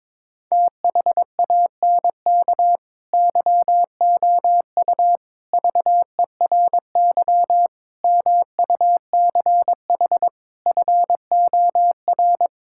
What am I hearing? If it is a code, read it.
Morse code, heard as a keyed tone, about 23 wpm, 700 Hz. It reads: THANK YOU VERY MUCH FOR
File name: practice / 22wpm / Sentences with 4-Word Phrases - 22wpm